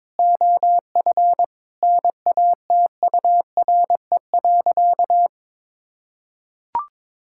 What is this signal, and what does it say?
Morse code, heard as a keyed tone, about 22 wpm, 700 Hz. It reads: OF NATURE.
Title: Sentences from Top 1000 Words - Review Part 6 - 22wpm